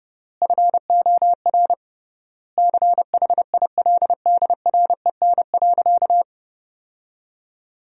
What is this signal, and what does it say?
Morse code, heard as a keyed tone, about 32 wpm, 700 Hz. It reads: FOR CHILDREN.
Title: 3x Word-Spacing - Sentences from Top 400 Words - 30wpm